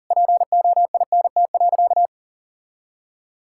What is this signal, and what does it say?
Morse code, heard as a keyed tone, about 40 wpm, 700 Hz. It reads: POINT.
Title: Sentences from Top 1000 Words - Review Part 5 - 40wpm